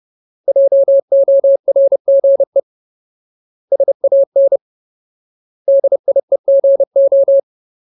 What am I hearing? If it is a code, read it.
Morse code, heard as a keyed tone, about 30 wpm, 550 Hz. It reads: JORGE SAN DIEGO
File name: Ultimate Word List / 30WPM Ultimate